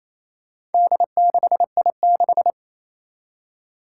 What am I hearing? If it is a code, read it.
Morse code, heard as a keyed tone, about 28 wpm, 700 Hz. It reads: D6I6
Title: Four Challenging Letter-Number Combinations 28wpm